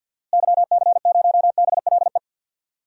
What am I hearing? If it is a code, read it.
Morse code, heard as a keyed tone, about 46 wpm, 700 Hz. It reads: YC0BLE
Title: DXCC-100 Call Signs - 6 Characters - Repeated 3x - 50wpm